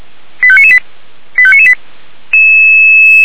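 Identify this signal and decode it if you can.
Five-tone selective call (ZVEI), heard as quick runs of five tones, tones of about 70 ms. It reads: 85998 85998
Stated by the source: ZVEI five-tone selective call received by SDR.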